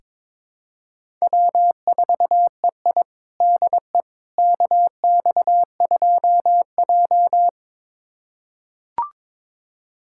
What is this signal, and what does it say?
Morse code, heard as a keyed tone, about 22 wpm, 700 Hz. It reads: W4EI DE KX2J
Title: USA Callsign DE Callsign - 4 Characters - 22wpm